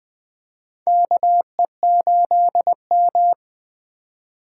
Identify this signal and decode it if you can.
Morse code, heard as a keyed tone, about 20 wpm, 700 Hz. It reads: KE8M
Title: K1USN SST QSO Practice - 20wpm